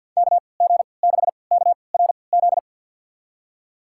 Morse code, heard as a keyed tone, about 51 wpm, spaced as for 28 wpm, 700 Hz. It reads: KG6KRZ
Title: USA Call Signs - 6 Characters - 3x Character Spacing - 50wpm